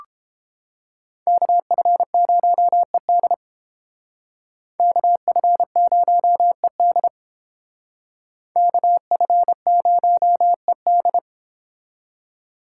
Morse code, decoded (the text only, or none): KF0ED KF0ED KF0ED